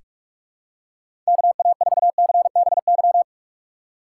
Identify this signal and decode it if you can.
Morse code, heard as a keyed tone, about 46 wpm, 700 Hz. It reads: KA4CBY
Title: USA Call Signs - 6 Characters - 2x Character Spacing - 45wpm